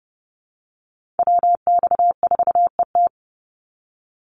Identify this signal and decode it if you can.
Morse code, heard as a keyed tone, about 30 wpm, 700 Hz. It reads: WX4ET
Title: NA SOTA Chaser Callsigns 30wpm